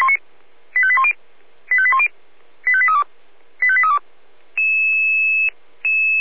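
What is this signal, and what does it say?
Five-tone selective call (ZVEI), heard as quick runs of five tones, tones of about 70 ms. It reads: 85810 85810 85832 85832
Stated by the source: ZVEI five-tone selective call received by SDR.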